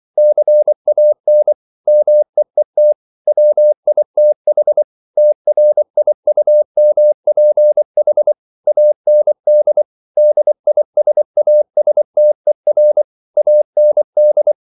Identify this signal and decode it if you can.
Morse code, heard as a keyed tone, about 24 wpm, 600 Hz. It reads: CAN MEET WITH TRIUMPH AND DISASTER AND
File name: CW / Practice / If24